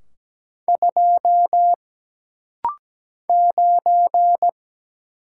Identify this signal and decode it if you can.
Morse code, heard as a keyed tone, about 17 wpm, 700 Hz. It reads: 2 9
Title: Single Letter-Number - Rapid-Fire 17wpm